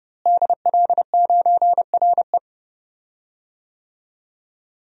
Morse code, heard as a keyed tone, about 30 wpm, 700 Hz. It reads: DL9RE
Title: DXCC-100 Call Signs - 5 Characters - Repeated 2x - 30wpm